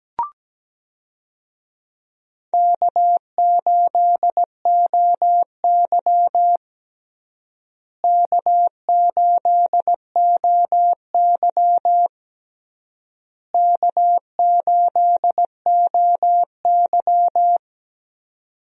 Morse code, heard as a keyed tone, about 17 wpm, 700 Hz. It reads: K8OY K8OY K8OY
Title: USA Call Signs - 4 Characters - Repeated 3x - 17wpm